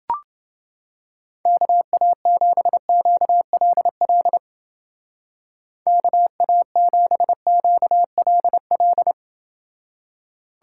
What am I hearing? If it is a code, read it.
Morse code, heard as a keyed tone, about 28 wpm, 700 Hz. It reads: KA7QLL KA7QLL